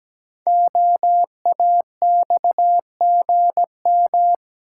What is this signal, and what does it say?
Morse code, heard as a keyed tone, about 17 wpm, 700 Hz. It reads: OAXGM
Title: Code Groups - 5 Letters - 17wpm